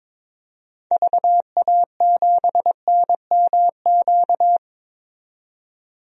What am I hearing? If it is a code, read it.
Morse code, heard as a keyed tone, about 22 wpm, 700 Hz. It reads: VA7NMQ
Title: DXCC-100 Call Signs - 6 Characters - 22wpm